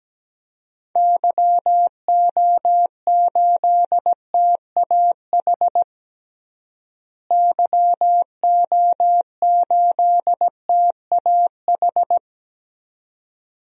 Morse code, heard as a keyed tone, about 17 wpm, 700 Hz. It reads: YO8TAH YO8TAH
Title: DXCC-100 Call Signs - 6 Characters - Repeated 2x - 17wpm